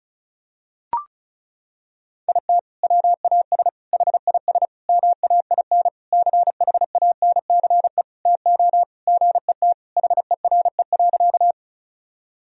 Morse code, heard as a keyed tone, about 34 wpm, 700 Hz. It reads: IT WAS HIS MAIN CHANCE TO GET HERE.
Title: Sentences from Top 1000 Words - Review Part 2 - 35wpm